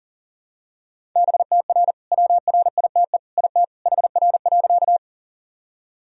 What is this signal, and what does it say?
Morse code, heard as a keyed tone, about 40 wpm, 700 Hz. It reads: BTR WRITE IT HR.
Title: Sentences from Top 400 Words and Abbreviations 40wpm